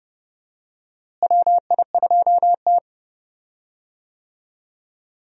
Morse code, heard as a keyed tone, about 31 wpm, 700 Hz. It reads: WI2T
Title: USA Call Signs - 4 Characters - Repeated 2x - 30wpm